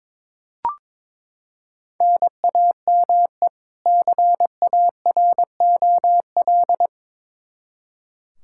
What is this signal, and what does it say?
Morse code, heard as a keyed tone, about 22 wpm, 700 Hz. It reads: NAME CAROL
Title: Practice QSO - Exchange 1 - Beginning - 22wpm